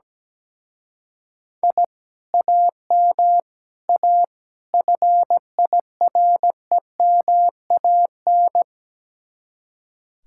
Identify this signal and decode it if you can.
Morse code, heard as a keyed tone, about 18 wpm, 700 Hz. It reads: I AM A FIREMAN
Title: Speed-Racing Common QSO Phrases - 17wpm